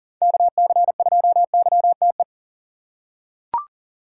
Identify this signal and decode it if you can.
Morse code, heard as a keyed tone, about 40 wpm, 700 Hz. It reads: KC2YTE
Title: USA Call Signs - 6 Characters - 4x Character Spacing - 40wpm